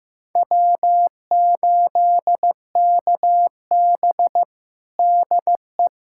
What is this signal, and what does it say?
Morse code, heard as a keyed tone, about 15 wpm, 700 Hz. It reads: W8KB DE
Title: USA Callsign DE Callsign - 4 Characters - 15wpm